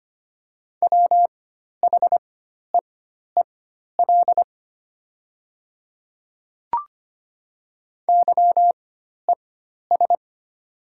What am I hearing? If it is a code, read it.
Morse code, heard as a keyed tone, about 25 wpm, spaced as for 12 wpm, 700 Hz. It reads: WHEEL YES
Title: 4x Character Spacing - Top 500 Words - 25wpm